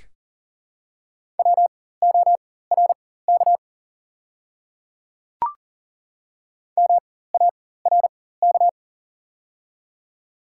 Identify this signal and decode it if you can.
Morse code, heard as a keyed tone, about 40 wpm, spaced as for 19 wpm, 690 Hz. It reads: WORK MARK
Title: 4x Character Spacing - Top 300 Words - 40wpm